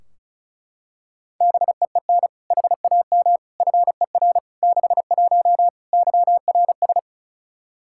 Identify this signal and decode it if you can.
Morse code, heard as a keyed tone, about 36 wpm, 700 Hz. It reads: BEEN HAM FER 61 YRS
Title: Practice QSO - Exchange 3 - More Info - 35wpm